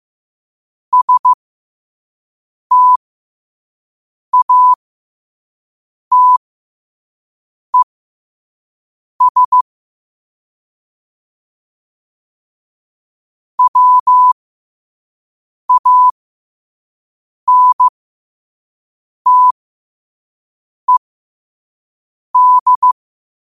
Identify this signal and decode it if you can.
Morse code, heard as a keyed tone, about 15 wpm, spaced as for 5 wpm, 1000 Hz. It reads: STATES WANTED
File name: qotdfiles / 5wpm